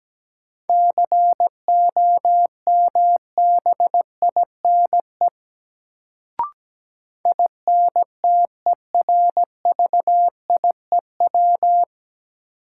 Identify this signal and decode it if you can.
Morse code, heard as a keyed tone, about 17 wpm, 700 Hz. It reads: COMBINE INTERVIEW
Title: Sentences from Top 1600 Words - New Word Drill Alt - 17wpm